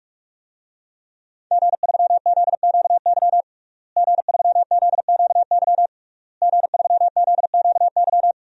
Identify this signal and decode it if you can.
Morse code, heard as a keyed tone, about 42 wpm, 700 Hz. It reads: G3ZQY G3ZQY G3ZQY
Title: DXCC-100 Call Signs - 5 Characters - Repeated 3x - 45wpm